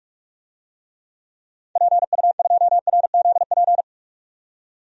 Morse code, heard as a keyed tone, about 41 wpm, 700 Hz. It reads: PU2RZP
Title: DXCC-100 Call Signs - 6 Characters - 45wpm